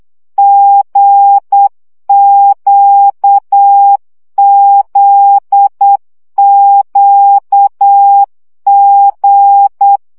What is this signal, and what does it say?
Morse code, heard as a keyed tone, about 8 wpm, 800 Hz. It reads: GQZQG
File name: CW gyakorlas g z q 7 baud